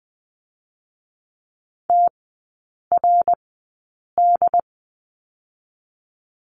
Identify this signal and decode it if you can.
Morse code, heard as a keyed tone, about 20 wpm, spaced as for 8 wpm, 700 Hz. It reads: TRD
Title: 067 Lesson 12e - Letter d - Three Characters 20wpm